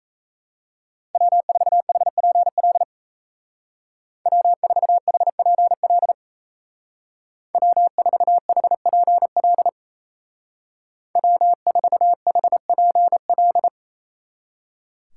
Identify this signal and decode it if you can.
Morse code, heard as a keyed tone, about 35 wpm, 700 Hz. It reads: W4HPL W4HPL W4HPL W4HPL
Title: Speed Racing - USA 1x3 Call Signs 28wpm